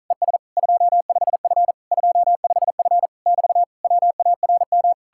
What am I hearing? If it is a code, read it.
Morse code, heard as a keyed tone, about 48 wpm, 700 Hz. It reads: ES 25F 25F <BT> WARM